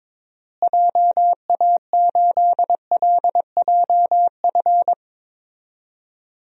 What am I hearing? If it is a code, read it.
Morse code, heard as a keyed tone, about 22 wpm, 700 Hz. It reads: JA8LJF
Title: DXCC-100 Call Signs - 6 Characters - Repeated 3x - 22wpm